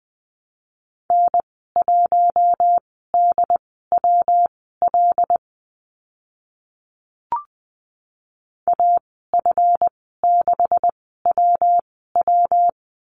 Morse code, heard as a keyed tone, about 20 wpm, spaced as for 14 wpm, 700 Hz. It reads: N1DWL AF6WW
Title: 140 Lesson 23g - Letter f - Call Signs 20wpm